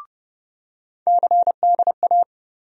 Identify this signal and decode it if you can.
Morse code, heard as a keyed tone, about 30 wpm, 700 Hz. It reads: CDA
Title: USA Call Sign Postfixes - 3 Letters 30wpm